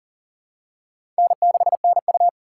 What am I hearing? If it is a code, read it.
Morse code, heard as a keyed tone, about 40 wpm, 700 Hz. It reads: N6NU
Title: USA Callsign DE Callsign - 4 Characters - 40wpm